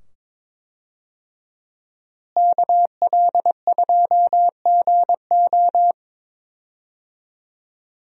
Morse code, heard as a keyed tone, about 22 wpm, 700 Hz. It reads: KL2GO
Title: USA Call Signs - 5 Characters - Repeated 2x - 22wpm